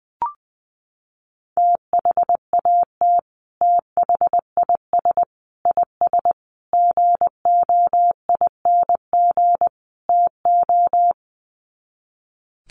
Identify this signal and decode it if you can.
Morse code, heard as a keyed tone, about 20 wpm, 700 Hz. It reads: THAT THIS IS GOING TO
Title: Sets of 5 Words Encore2 20wpm